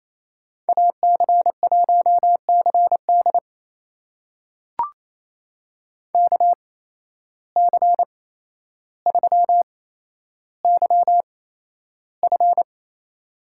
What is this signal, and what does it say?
Morse code, heard as a keyed tone, about 28 wpm, spaced as for 12 wpm, 700 Hz. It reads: AC1CD KC3YF
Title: USA Call Signs - 5 Characters - 8x Character Spacing - 28wpm